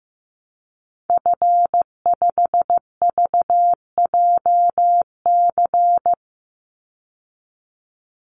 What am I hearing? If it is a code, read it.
Morse code, heard as a keyed tone, about 14 wpm, 700 Hz. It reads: F5VJC
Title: Call Signs Repeated 3x - 15wpm